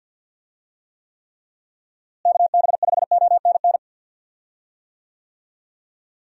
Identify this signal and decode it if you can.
Morse code, heard as a keyed tone, about 47 wpm, 700 Hz. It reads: KB5ONN
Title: USA Call Signs - 6 Characters - Repeated 3x - 50wpm